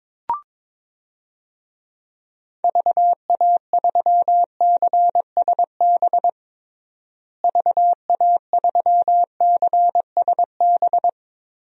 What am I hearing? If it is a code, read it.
Morse code, heard as a keyed tone, about 22 wpm, 700 Hz. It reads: VA3CSB VA3CSB
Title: DXCC-100 Call Signs - 6 Characters - Repeated 3x - 22wpm